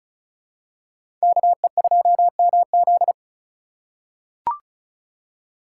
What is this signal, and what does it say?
Morse code, heard as a keyed tone, about 34 wpm, 700 Hz. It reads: KE2MZ
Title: USA Call Signs - 4x Character Spacing - 35wpm